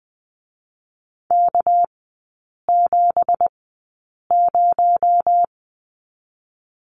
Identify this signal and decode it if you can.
Morse code, heard as a keyed tone, about 19 wpm, spaced as for 8 wpm, 700 Hz. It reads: K70
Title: 218 Lesson 35e - Number 0 - Three Characters 20wpm